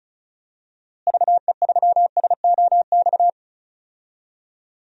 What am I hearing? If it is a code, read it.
Morse code, heard as a keyed tone, about 36 wpm, 700 Hz. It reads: VE3SOX
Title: DXCC-100 Call Signs - 6 Characters - Repeated 3x - 35wpm